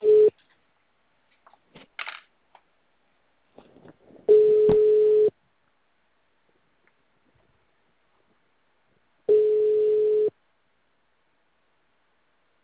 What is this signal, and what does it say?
Ringback tone (Europe, 425 Hz).